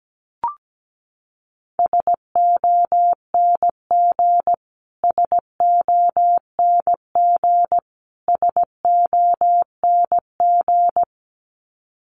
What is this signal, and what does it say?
Morse code, heard as a keyed tone, about 17 wpm, 700 Hz. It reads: SONG SONG SONG
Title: Top 500 Words Repeated 17wpm